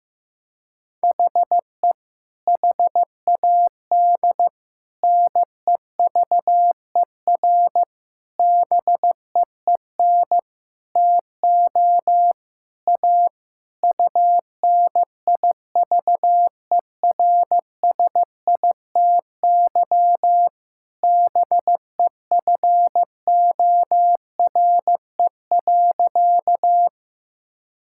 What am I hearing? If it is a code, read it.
Morse code, heard as a keyed tone, about 15 wpm, 700 Hz. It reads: HE HAD NEVER BEEN TO A UNIVERSITY BEFORE.
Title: Sentences from Top 1000 Words - Review Part 5 - 15wpm